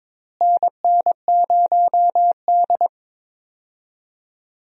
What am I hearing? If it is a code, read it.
Morse code, heard as a keyed tone, about 22 wpm, 700 Hz. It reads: NN0D
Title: USA Call Signs - 4 Characters - Repeated 2x - 22wpm